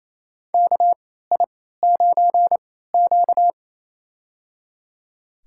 Morse code, heard as a keyed tone, about 28 wpm, spaced as for 16 wpm, 700 Hz. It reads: KI9Q
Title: USA Call Signs - 4 Characters - 3x Character Spacing - 28wpm